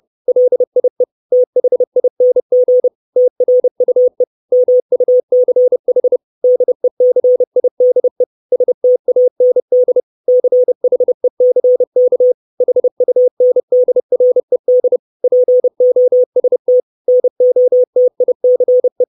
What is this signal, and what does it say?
Morse code, heard as a keyed tone, about 30 wpm, 500 Hz. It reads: LIE THING TRUE MUCH DECIDE STAND CHECK HUNDRED POST NOTICE
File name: 632 Words 30WPM